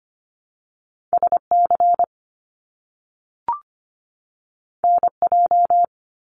Morse code, heard as a keyed tone, about 25 wpm, 700 Hz. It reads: SC NJ
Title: US States 25wpm